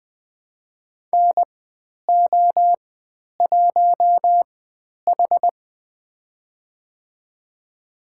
Morse code, heard as a keyed tone, about 20 wpm, spaced as for 10 wpm, 700 Hz. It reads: NO1H
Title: 095 Lesson 16g - Letter c - Call Signs 20wpm